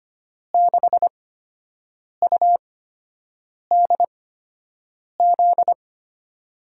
Morse code, heard as a keyed tone, about 25 wpm, spaced as for 7 wpm, 700 Hz. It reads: 6UDZ
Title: USA Call Signs - 6 Characters - 8x Character Spacing - 25wpm